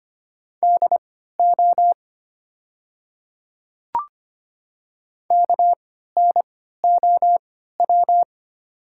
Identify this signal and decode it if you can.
Morse code, heard as a keyed tone, about 25 wpm, spaced as for 14 wpm, 700 Hz. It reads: DO KNOW
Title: 3x Character Spacing - Top 100 Words - 25wpm